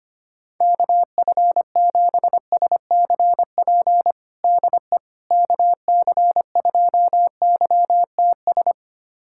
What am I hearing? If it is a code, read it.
Morse code, heard as a keyed tone, about 25 wpm, 700 Hz. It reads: KF7SCP DE KC2YTS
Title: USA Callsign DE Callsign - 6 Characters - 25wpm